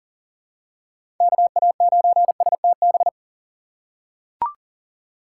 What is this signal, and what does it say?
Morse code, heard as a keyed tone, about 40 wpm, 700 Hz. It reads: KA9STB
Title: USA Call Signs - 6 Characters - 3x Character Spacing - 40wpm